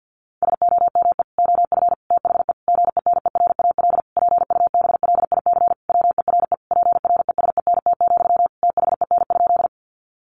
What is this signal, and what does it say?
Morse code, heard as a keyed tone, about 50 wpm, 700 Hz. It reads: SOME OF THE GENERAL PUBLIC WERE PRESENT, THEN?
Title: Sentences from Top 800 Words 50wpm